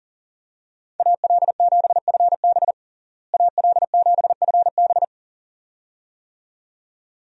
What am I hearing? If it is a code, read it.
Morse code, heard as a keyed tone, about 40 wpm, 700 Hz. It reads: AL7FB AL7FB
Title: USA Call Signs - 5 Characters - Repeated 2x - 40wpm